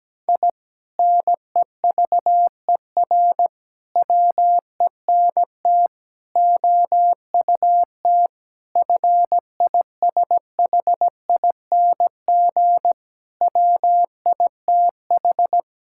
Morse code, heard as a keyed tone, about 17 wpm, 700 Hz. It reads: I NEVER WENT OUT FISHING WITH
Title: Sentences from Top 1000 Words - Review Part 5 - 17wpm